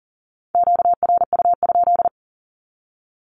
Morse code, heard as a keyed tone, about 40 wpm, 700 Hz. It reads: QRU?
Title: Q-Codes 40wpm